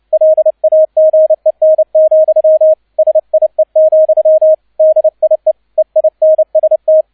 Morse code, heard as a keyed tone, about 29 wpm, 630 Hz. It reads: LAGEN, SIE, DIE EINST
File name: 002 Zweiter Streich 30wpm